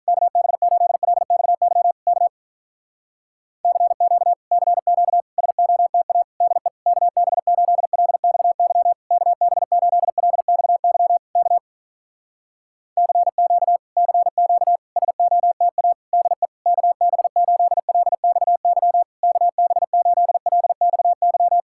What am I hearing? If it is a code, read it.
Morse code, heard as a keyed tone, about 48 wpm, 700 Hz. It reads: KB8LXY K CQ CQ SOTA DE KB8LXY KB8LXY K CQ CQ SOTA DE KB8LXY KB8LXY